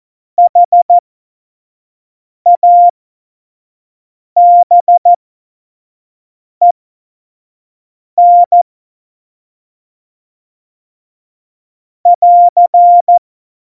Morse code, heard as a keyed tone, about 14 wpm, spaced as for 5 wpm, 700 Hz. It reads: HABEN <AR>
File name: Morsekurs / Witze / Witz 08. Mai 2018, 25 F70